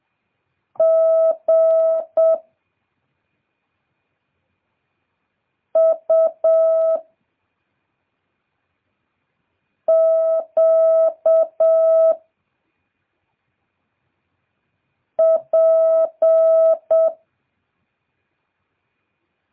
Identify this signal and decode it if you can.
Morse code, heard as a keyed tone, about 7 wpm, 650 Hz. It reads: G U Q P